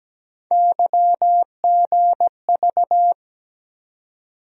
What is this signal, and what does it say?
Morse code, heard as a keyed tone, about 17 wpm, 700 Hz. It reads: YGV
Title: USA Call Sign Postfixes - 3 Letters 17wpm